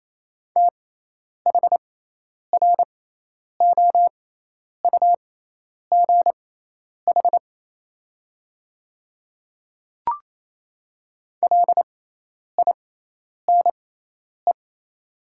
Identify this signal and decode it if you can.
Morse code, heard as a keyed tone, about 28 wpm, spaced as for 10 wpm, 700 Hz. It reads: THROUGH LINE
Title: 6x Character Spacing - Top 200 Words - 28wpm